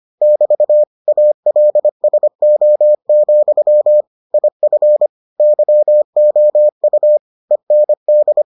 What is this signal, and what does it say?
Morse code, heard as a keyed tone, about 25 wpm, 600 Hz. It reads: <BT> ALSO, IF YOU END